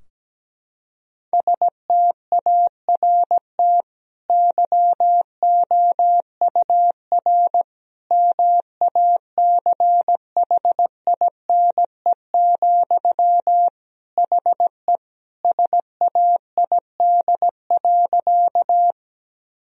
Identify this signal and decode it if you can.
Morse code, heard as a keyed tone, about 17 wpm, 700 Hz. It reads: START YOUR MACHINE, HE SAID.
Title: Sentences from Top 1000 Words - Review Part 2 - 17wpm